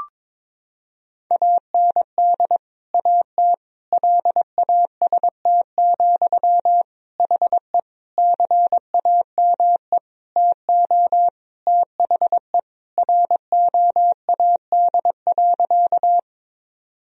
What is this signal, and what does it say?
Morse code, heard as a keyed tone, about 22 wpm, 700 Hz. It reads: AND AT LAST, HE CAME TO THE ROAD.
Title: Sentences from Top 1000 Words - Review Part 2 - 22wpm